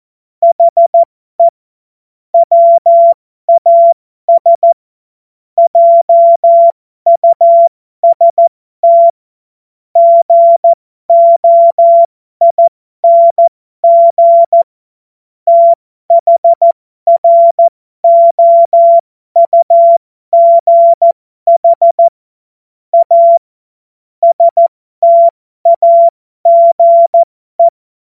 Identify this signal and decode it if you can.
Morse code, heard as a keyed tone, about 14 wpm, spaced as for 12 wpm, 680 Hz. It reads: HE WAS JUST GOING THROUGH A STAGE
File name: Morsekurs / Witze